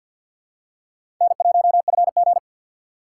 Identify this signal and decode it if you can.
Morse code, heard as a keyed tone, about 49 wpm, 700 Hz. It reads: N1FG
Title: USA Call Signs - 4 Characters - 2x Character Spacing - 50wpm